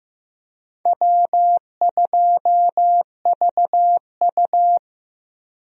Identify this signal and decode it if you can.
Morse code, heard as a keyed tone, about 15 wpm, 700 Hz. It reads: W2VU
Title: USA Call Signs - 4 Characters - Repeated 2x - 15wpm